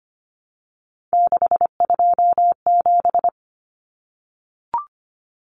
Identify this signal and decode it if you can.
Morse code, heard as a keyed tone, about 25 wpm, 700 Hz. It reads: 627
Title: Common Cut Numbers - 3-Digit 25wpm